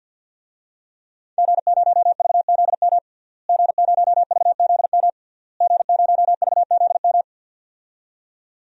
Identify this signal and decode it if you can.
Morse code, heard as a keyed tone, about 50 wpm, 700 Hz. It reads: G0VZM G0VZM G0VZM
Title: DXCC-100 Call Signs - 5 Characters - Repeated 3x - 50wpm